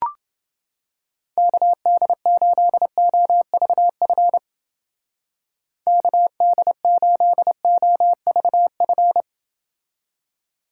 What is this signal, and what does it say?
Morse code, heard as a keyed tone, about 29 wpm, 700 Hz. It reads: KD8OVF KD8OVF